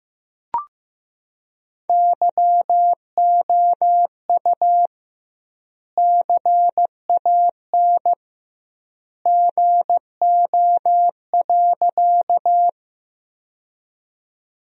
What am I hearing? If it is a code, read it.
Morse code, heard as a keyed tone, about 15 wpm, 700 Hz. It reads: YOU CAN GO.
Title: 2x Word-Spacing - Sentences from Top 100 Words - 15wpm